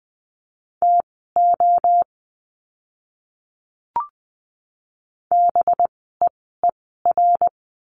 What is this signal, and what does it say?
Morse code, heard as a keyed tone, about 20 wpm, spaced as for 14 wpm, 700 Hz. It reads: TO BEER
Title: 185 Lesson 30f - Letter b - Words 20wpm